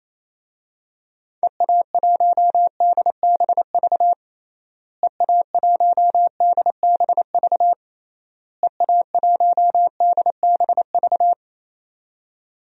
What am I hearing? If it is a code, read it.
Morse code, heard as a keyed tone, about 28 wpm, 700 Hz. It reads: EA1DBV EA1DBV EA1DBV
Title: DXCC-100 Call Signs - 6 Characters - Repeated 3x - 28wpm